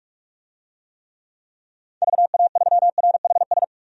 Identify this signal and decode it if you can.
Morse code, heard as a keyed tone, about 46 wpm, 700 Hz. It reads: VA3RHS
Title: DXCC-100 Call Signs - 6 Characters - Repeated 2x - 45wpm